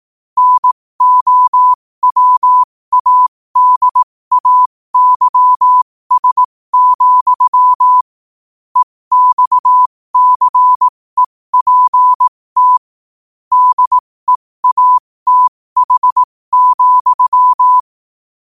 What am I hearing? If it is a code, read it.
Morse code, heard as a keyed tone, about 18 wpm, spaced as for 15 wpm, 1000 Hz. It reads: NOWADAYS, EXCEPT DEATH,
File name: qotdfiles / 15wpm